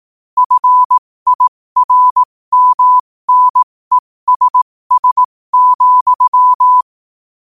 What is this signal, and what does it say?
Morse code, heard as a keyed tone, about 18 wpm, spaced as for 15 wpm, 1000 Hz. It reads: FIRMNESS,
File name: qotdfiles / 15wpm